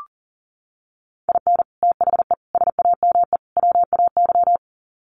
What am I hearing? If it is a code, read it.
Morse code, heard as a keyed tone, about 40 wpm, 700 Hz. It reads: IN THE SAME WAY
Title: Sets of 4 Words 40wpm